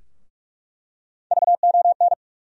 Morse code, heard as a keyed tone, about 45 wpm, 700 Hz. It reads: VON
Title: USA Call Sign Postfixes - 3 Letters 45wpm